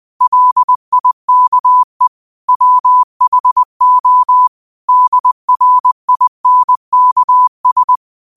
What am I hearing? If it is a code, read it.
Morse code, heard as a keyed tone, about 20 wpm, 1000 Hz. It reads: LIKE WHO DRINKS
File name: qotdfiles / 20wpm